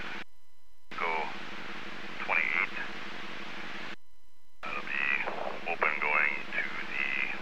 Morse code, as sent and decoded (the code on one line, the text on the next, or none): none